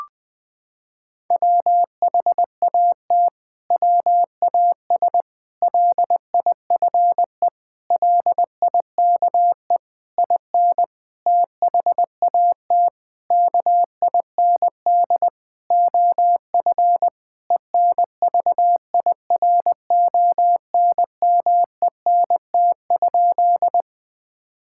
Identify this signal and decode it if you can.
Morse code, heard as a keyed tone, about 21 wpm, 700 Hz. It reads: WHAT WAS LIFE LIKE IN THAT KIND OF ENVIRONMENT?
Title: Sentences from Top 1000 Words - Review Part 5 - 20wpm